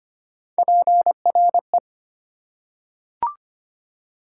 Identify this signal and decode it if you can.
Morse code, heard as a keyed tone, about 25 wpm, 700 Hz. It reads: PRE
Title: Prefixes and Suffixes 25wpm